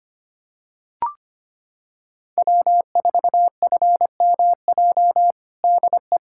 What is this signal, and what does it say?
Morse code, heard as a keyed tone, about 25 wpm, 700 Hz. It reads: W4FMJ DE
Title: USA Callsign DE Callsign - 5 Characters - 25wpm